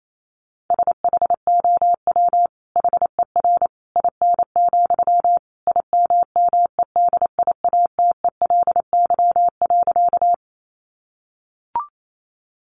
Morse code, heard as a keyed tone, about 28 wpm, 700 Hz. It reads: SHOW HER IN, IMMEDIATELY.
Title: Sentences from Top 1100 Words 28wpm